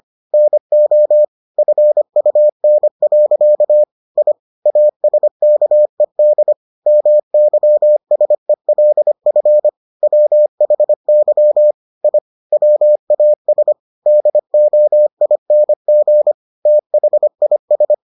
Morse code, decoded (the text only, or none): NO FUN. I ASKED MYSELF WHY I WAS DOING THIS